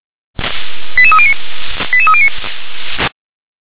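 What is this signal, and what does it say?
Five-tone selective call (ZVEI), heard as quick runs of five tones, tones of about 70 ms. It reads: 88309 88309
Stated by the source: ZVEI five-tone selective call received on an SDR.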